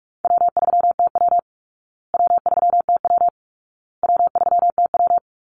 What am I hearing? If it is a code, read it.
Morse code, heard as a keyed tone, about 44 wpm, 700 Hz. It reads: W3TW W3TW W3TW
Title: Easy Call Signs Repeated 3x - 45wpm